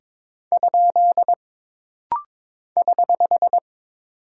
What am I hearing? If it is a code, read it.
Morse code, heard as a keyed tone, about 22 wpm, 700 Hz. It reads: ? <HH>
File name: practice / 22wpm / Single Punctuation-Prosign - Rapid-Fire - 22wpm